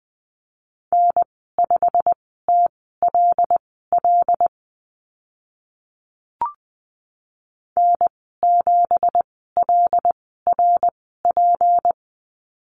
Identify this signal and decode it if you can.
Morse code, heard as a keyed tone, about 20 wpm, spaced as for 14 wpm, 700 Hz. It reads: N5TLL N7LRP
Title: 167 Lesson 27g - Number 7 - Call Signs 20wpm